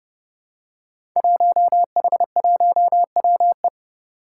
Morse code, heard as a keyed tone, about 29 wpm, 700 Hz. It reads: 1H1WE
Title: Five Challenging Letter-Number Combinations 30wpm